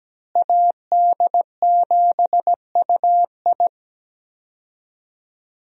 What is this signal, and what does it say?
Morse code, heard as a keyed tone, about 17 wpm, 700 Hz. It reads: AD7UI
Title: USA Call Signs - 5 Characters - Repeated 3x - 17wpm